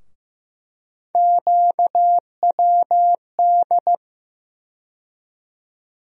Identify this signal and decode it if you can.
Morse code, heard as a keyed tone, about 15 wpm, 700 Hz. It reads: QWD
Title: USA Call Sign Postfixes - 3 Letters 15wpm